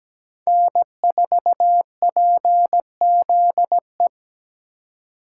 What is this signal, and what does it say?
Morse code, heard as a keyed tone, about 17 wpm, 700 Hz. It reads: N4PZE
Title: USA Call Signs - 5 Characters - 2x Character Spacing - 17wpm